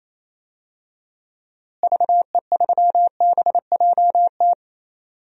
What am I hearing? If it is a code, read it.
Morse code, heard as a keyed tone, about 28 wpm, 700 Hz. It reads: VE3BJT
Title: DXCC-100 Call Signs - 6 Characters - Repeated 3x - 28wpm